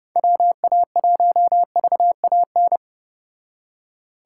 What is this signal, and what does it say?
Morse code, heard as a keyed tone, about 29 wpm, 700 Hz. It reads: WA1VAN